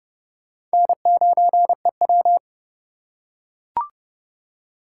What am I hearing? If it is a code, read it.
Morse code, heard as a keyed tone, about 30 wpm, 700 Hz. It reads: N9EW
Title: USA Call Signs - 2x Character Spacing - 30wpm